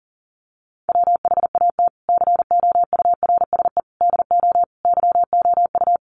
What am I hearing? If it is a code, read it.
Morse code, heard as a keyed tone, about 40 wpm, 700 Hz. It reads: WHAT COURSE DO YOU
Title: Sentences from Top 1200 Words 40wpm